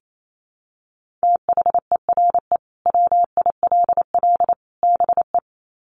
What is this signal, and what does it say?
Morse code, heard as a keyed tone, about 28 wpm, 700 Hz. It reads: THERE WILL BE
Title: Sets of 3 Words 28wpm